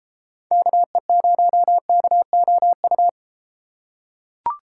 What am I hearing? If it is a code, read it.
Morse code, heard as a keyed tone, about 32 wpm, 700 Hz. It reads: KE0KOU